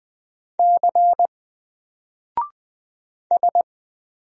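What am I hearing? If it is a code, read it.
Morse code, heard as a keyed tone, about 20 wpm, 700 Hz. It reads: C S
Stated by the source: Single letter and number Morse drill at 20 wpm, from N4LCD.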